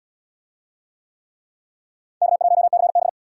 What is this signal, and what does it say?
Morse code, heard as a keyed tone, about 71 wpm, 700 Hz. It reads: K1CB